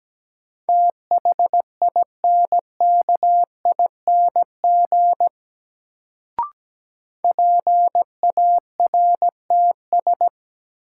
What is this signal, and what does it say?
Morse code, heard as a keyed tone, about 17 wpm, 700 Hz. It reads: THINKING PARTS
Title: Sentences from Top 1000 Words - Review Part 6 - New Word Drill - 17wpm